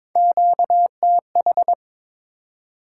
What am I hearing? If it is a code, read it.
Morse code, heard as a keyed tone, about 22 wpm, 700 Hz. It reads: QTH
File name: Top 20 QSO Elements - 22wpm